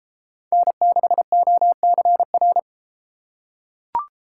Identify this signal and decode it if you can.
Morse code, heard as a keyed tone, about 33 wpm, 700 Hz. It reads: N6OCR